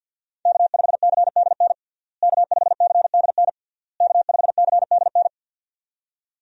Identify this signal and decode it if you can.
Morse code, heard as a keyed tone, about 51 wpm, 700 Hz. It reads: K5CDN K5CDN K5CDN
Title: USA Call Signs - 5 Characters - Repeated 3x - 50wpm